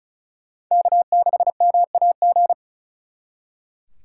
Morse code, heard as a keyed tone, about 45 wpm, 700 Hz. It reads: K6MAG